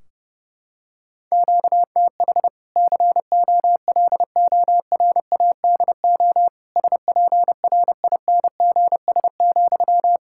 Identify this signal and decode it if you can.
Morse code, heard as a keyed tone, about 30 wpm, 700 Hz. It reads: QTH COLORADO SPRINGS,
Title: Practice QSO - Exchange 1 - Beginning - 30wpm